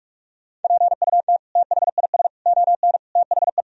Morse code, heard as a keyed tone, about 41 wpm, 700 Hz. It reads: PUT THIS ON THE
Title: Sentences from Top 1000 Words - Review Part 1 - 45wpm